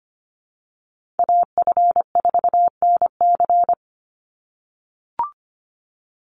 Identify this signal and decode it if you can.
Morse code, heard as a keyed tone, about 25 wpm, 700 Hz. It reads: AF4NC
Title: Call Signs - 25wpm